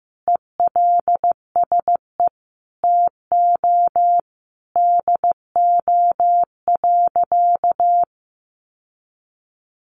Morse code, heard as a keyed tone, about 15 wpm, 700 Hz. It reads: ELSE TO DO.
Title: Sentences from Top 800 Words 15wpm